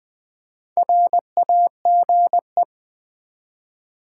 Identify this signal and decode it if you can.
Morse code, heard as a keyed tone, about 20 wpm, 700 Hz. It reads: RAGE
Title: Build A Word - 2 Letters At A Time - 4-Letter Words - 20wpm